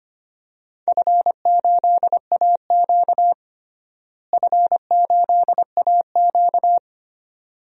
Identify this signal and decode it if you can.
Morse code, heard as a keyed tone, about 25 wpm, 700 Hz. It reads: F8AQ F8AQ
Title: DXCC-100 Call Signs - 4 Characters - Repeated 2x - 25wpm